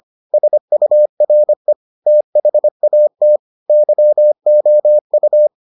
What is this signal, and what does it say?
Morse code, heard as a keyed tone, about 25 wpm, 600 Hz. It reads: SURE THAT YOU